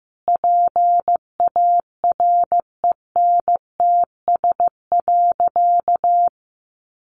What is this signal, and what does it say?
Morse code, heard as a keyed tone, about 15 wpm, 700 Hz. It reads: PARENTS.
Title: Easy Phrases 15wpm